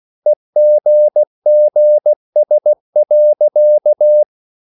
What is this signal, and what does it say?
Morse code, heard as a keyed tone, about 16 wpm, 600 Hz. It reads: EGGS.
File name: CW / Practice / EnglishSayings16